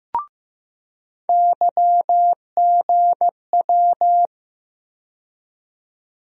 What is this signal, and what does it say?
Morse code, heard as a keyed tone, about 15 wpm, 700 Hz. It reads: YGW
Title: USA Call Sign Postfixes - 3 Letters 15wpm